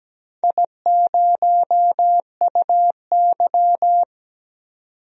Morse code, heard as a keyed tone, about 17 wpm, 700 Hz. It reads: I0UY
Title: DXCC-100 Call Signs - 4 Characters - Repeated 3x - 17wpm